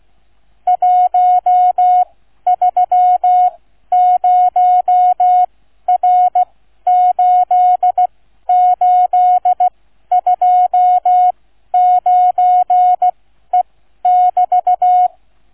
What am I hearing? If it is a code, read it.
Morse code, heard as a keyed tone, about 16 wpm, spaced as for 12 wpm, 730 Hz. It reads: 130R8829E<BT>
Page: QRS International Flight Contest 2025 Demo Flight NR 00 Sound files Listen-to or download incoming QNC sound file